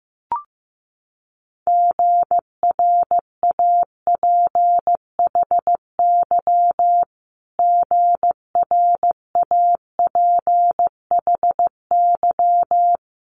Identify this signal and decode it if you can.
Morse code, heard as a keyed tone, about 15 wpm, 700 Hz. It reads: GRAPHY GRAPHY
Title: Multi-letter Phonemes Repeated 3x 15wpm